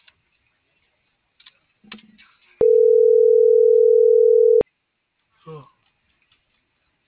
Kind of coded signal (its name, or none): telephone call progress tone